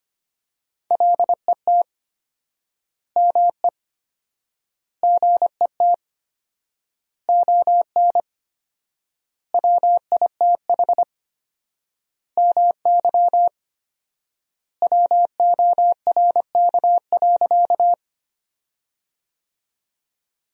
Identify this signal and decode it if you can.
Morse code, heard as a keyed tone, about 25 wpm, 700 Hz. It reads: LET ME GET ON WITH MY WORK.
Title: 4x Word-Spacing - Sentences from Top 300 Words - 25wpm